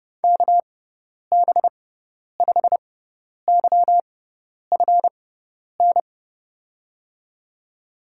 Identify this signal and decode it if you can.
Morse code, heard as a keyed tone, about 32 wpm, spaced as for 10 wpm, 700 Hz. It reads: KB5YFN
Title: USA Call Signs - 6 Characters - 6x Character Spacing - 30wpm